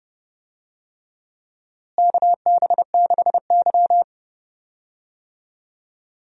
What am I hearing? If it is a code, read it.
Morse code, heard as a keyed tone, about 30 wpm, 700 Hz. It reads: KB6Y
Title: USA Call Signs - 4 Characters - Repeated 3x - 30wpm